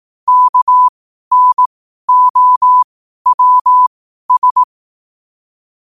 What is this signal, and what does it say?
Morse code, heard as a keyed tone, about 18 wpm, spaced as for 12 wpm, 1000 Hz. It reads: KNOWS
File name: qotdfiles / 12wpm